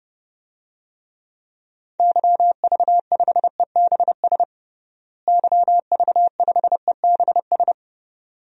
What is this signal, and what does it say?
Morse code, heard as a keyed tone, about 30 wpm, 700 Hz. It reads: YV5EBS YV5EBS
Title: DXCC-100 Call Signs - 6 Characters - Repeated 2x - 30wpm